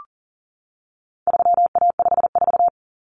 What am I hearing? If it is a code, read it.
Morse code, heard as a keyed tone, about 40 wpm, 700 Hz. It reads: 3A54
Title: Common Cut Numbers - 4-Digit 40wpm